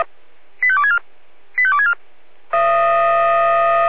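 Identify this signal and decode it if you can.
Five-tone selective call (ZVEI), heard as quick runs of five tones, tones of about 70 ms. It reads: 85264 85264
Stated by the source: ZVEI five-tone call received on SDR.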